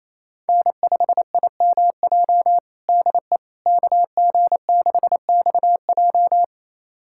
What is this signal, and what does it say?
Morse code, heard as a keyed tone, about 28 wpm, 700 Hz. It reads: N5IMJ DE KG6XJ
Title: USA Callsign DE Callsign - 5 Characters - 28wpm